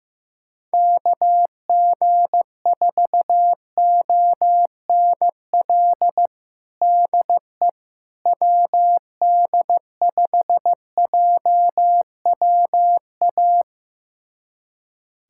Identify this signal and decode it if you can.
Morse code, heard as a keyed tone, about 15 wpm, 700 Hz. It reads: KG4ONL DE WD5JWA
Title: USA Callsign DE Callsign - 6 Characters - 15wpm